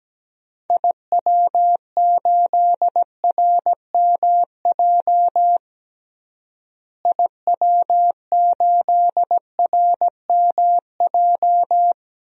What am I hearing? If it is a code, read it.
Morse code, heard as a keyed tone, about 17 wpm, 700 Hz. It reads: IW8RMJ IW8RMJ
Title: DXCC-100 Call Signs - 6 Characters - Repeated 2x - 17wpm